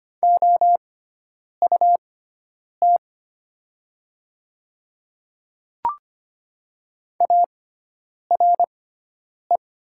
Morse code, heard as a keyed tone, about 25 wpm, spaced as for 9 wpm, 700 Hz. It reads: OUT ARE
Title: 6x Character Spacing - Top 100 Words - 25wpm